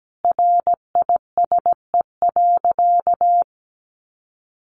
Morse code, heard as a keyed tone, about 17 wpm, 700 Hz. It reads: RISE.
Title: Sentences from Top 1300 Words 17wpm